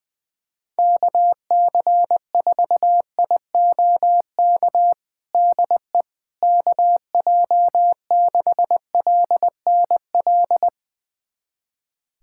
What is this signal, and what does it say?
Morse code, heard as a keyed tone, about 20 wpm, 700 Hz. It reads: KC4IOK DE KJ6LNL
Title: USA Callsign DE Callsign - 6 Characters - 20wpm